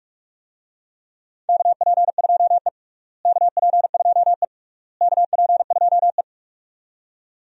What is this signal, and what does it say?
Morse code, heard as a keyed tone, about 41 wpm, 690 Hz. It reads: KP2E KP2E KP2E
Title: DXCC-100 Call Signs - 4 Characters - Repeated 3x - 45wpm